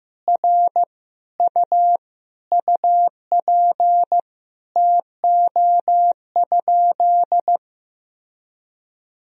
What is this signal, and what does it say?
Morse code, heard as a keyed tone, about 15 wpm, 700 Hz. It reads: R U UP TO?
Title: Sentences from Top 100 Words and Abbreviations 15wpm